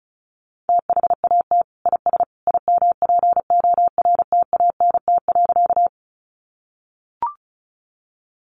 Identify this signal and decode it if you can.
Morse code, heard as a keyed tone, about 34 wpm, 700 Hz. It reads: THAT IS IMPORTANT.
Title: Sentences from Top 600 Words 35wpm